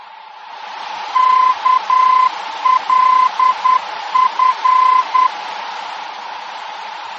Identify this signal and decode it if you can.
Morse code, heard as a keyed tone, about 10 wpm, 1020 Hz. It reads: KLF
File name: germany / mx / KLF-D-VOR